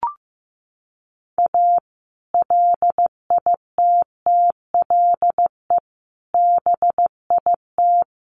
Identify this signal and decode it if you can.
Morse code, heard as a keyed tone, about 15 wpm, 700 Hz. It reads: A LITTLE BIT
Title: Sets of 4 Words 15wpm